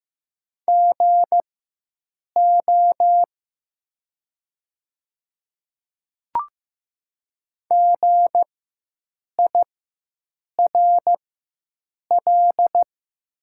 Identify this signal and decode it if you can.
Morse code, heard as a keyed tone, about 15 wpm, spaced as for 7 wpm, 700 Hz. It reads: GO GIRL
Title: 4x Character Spacing - Top 500 Words - 15wpm